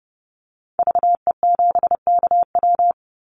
Morse code, heard as a keyed tone, about 30 wpm, 700 Hz. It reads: VE7KW
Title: Call Signs Repeated 2x - 30wpm